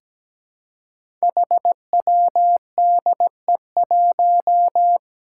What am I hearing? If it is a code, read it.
Morse code, heard as a keyed tone, about 17 wpm, 700 Hz. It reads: HWDE1
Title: Five Challenging Letter-Number Combinations 17wpm